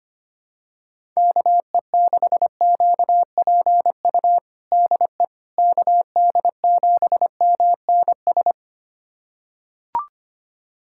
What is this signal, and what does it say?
Morse code, heard as a keyed tone, about 25 wpm, 700 Hz. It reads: KE6QPU DE KD7MNS
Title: USA Callsign DE Callsign 25wpm